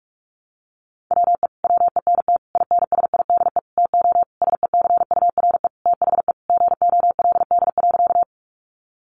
Morse code, heard as a keyed tone, about 44 wpm, 700 Hz. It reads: WE WENT INSIDE TO SECURE THE GOLD.
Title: Sentences from Top 1200 Words 45wpm